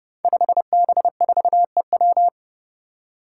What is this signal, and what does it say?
Morse code, heard as a keyed tone, about 30 wpm, 700 Hz. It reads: 5B4EW
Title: Five Challenging Letter-Number Combinations 30wpm